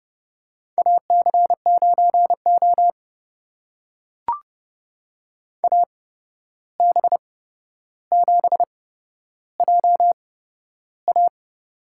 Morse code, heard as a keyed tone, about 30 wpm, spaced as for 12 wpm, 700 Hz. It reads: AC9O AB7JA
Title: USA Call Signs - 8x Character Spacing - 30wpm